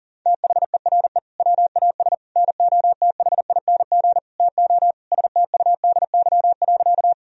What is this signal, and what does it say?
Morse code, heard as a keyed tone, about 40 wpm, 700 Hz. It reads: THERE WAS NOTHING TO STUDY.
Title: Sentences from Top 1000 Words - Review Part 1 - 40wpm